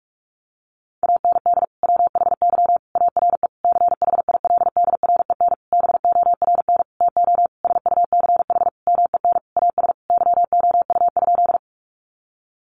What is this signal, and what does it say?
Morse code, heard as a keyed tone, about 44 wpm, 700 Hz. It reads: AND WHY ARE CHILDREN BORN TO SUCH MEN AS YOU?
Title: Sentences from Top 1000 Words 45wpm